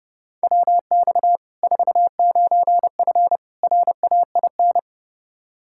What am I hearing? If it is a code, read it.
Morse code, heard as a keyed tone, about 30 wpm, 700 Hz. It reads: WX 49F RAIN
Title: Practice QSO - Exchange 2 - Info - 30wpm